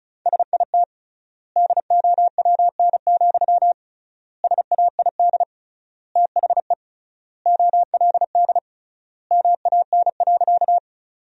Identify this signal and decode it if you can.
Morse code, heard as a keyed tone, about 34 wpm, 700 Hz. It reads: SIT DOWN, SAID THE OLD MAN.
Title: 3x Word-Spacing - Sentences from Top 500 Words - 35wpm